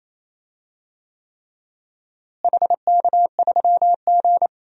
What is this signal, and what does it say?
Morse code, heard as a keyed tone, about 28 wpm, 700 Hz. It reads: HK3G
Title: DXCC-100 Call Signs - 4 Characters - Repeated 3x - 28wpm